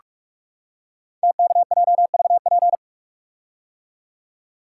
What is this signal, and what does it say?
Morse code, heard as a keyed tone, about 44 wpm, 690 Hz. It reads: TKJVP
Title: Code Groups - 5 Letters - 45wpm